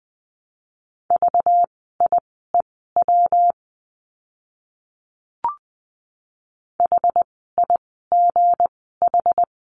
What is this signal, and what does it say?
Morse code, heard as a keyed tone, about 20 wpm, spaced as for 14 wpm, 700 Hz. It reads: VIEW HIGH
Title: 192 Lesson 31f - Letter v - Words 20wpm